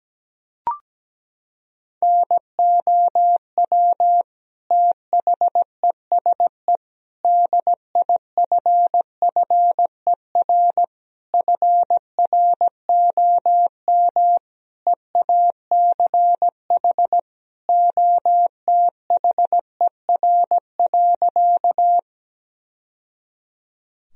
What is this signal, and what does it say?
Morse code, heard as a keyed tone, about 17 wpm, 700 Hz. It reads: NOW THESE DIFFER FROM EACH OTHER.
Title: Sentences from Top 1000 Words - Review Part 1 - 17wpm